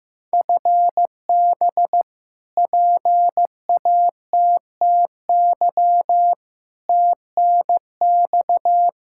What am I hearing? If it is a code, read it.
Morse code, heard as a keyed tone, about 15 wpm, 700 Hz. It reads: FB PATTY TNX
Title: K1USN SST QSO Practice 15wpm